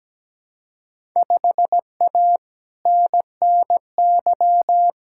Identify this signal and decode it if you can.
Morse code, heard as a keyed tone, about 17 wpm, 700 Hz. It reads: 5A NNY
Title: Field Day Exchanges - 17wpm